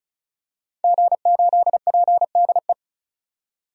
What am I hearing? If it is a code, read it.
Morse code, heard as a keyed tone, about 34 wpm, 700 Hz. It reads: G8PDE
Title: DXCC-100 Call Signs - 5 Characters - 35wpm